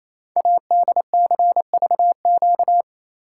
Morse code, heard as a keyed tone, about 29 wpm, 700 Hz. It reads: ADCVQ
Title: Code Groups - 5 Letters - 28wpm